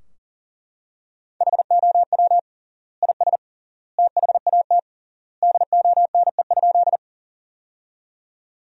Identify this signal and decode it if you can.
Morse code, heard as a keyed tone, about 40 wpm, 700 Hz. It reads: HOW IS THAT DONE?
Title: 3x Word-Spacing - Sentences from Top 500 Words - 40wpm